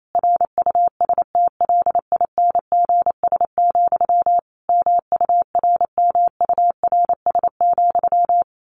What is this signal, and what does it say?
Morse code, heard as a keyed tone, about 28 wpm, 700 Hz. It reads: RUSTLINGS, MURMURS,
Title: A Sound Of Thunder 28wpm